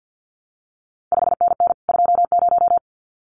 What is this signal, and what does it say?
Morse code, heard as a keyed tone, about 50 wpm, 700 Hz. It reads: 5NN 20
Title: CQ WW DX Contest Exchange - 50wpm